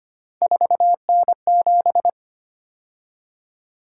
Morse code, heard as a keyed tone, about 25 wpm, 700 Hz. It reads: 4N7
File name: Three Letter-Number Combinations - 25wpm